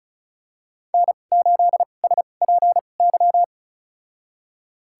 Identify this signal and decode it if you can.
Morse code, heard as a keyed tone, about 36 wpm, spaced as for 25 wpm, 700 Hz. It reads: N8SPY
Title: USA Call Signs - 5 Characters - 2x Character Spacing - 35wpm